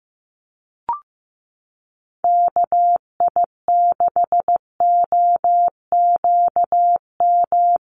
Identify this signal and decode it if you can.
Morse code, heard as a keyed tone, about 15 wpm, 700 Hz. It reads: KI6OQM
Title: USA Call Signs Repeated 3x 15wpm